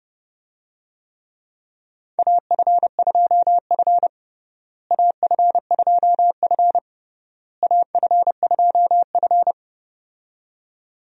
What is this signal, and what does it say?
Morse code, heard as a keyed tone, about 30 wpm, 700 Hz. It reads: AF2F AF2F AF2F
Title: USA Call Signs - 4 Characters - Repeated 3x - 30wpm